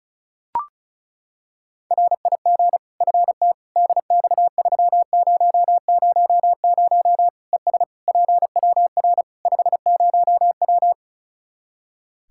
Morse code, heard as a keyed tone, about 34 wpm, 700 Hz. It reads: RIG FT DX3000 ES PWR 50W
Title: Practice QSO - Exchange 2 - Info - 35wpm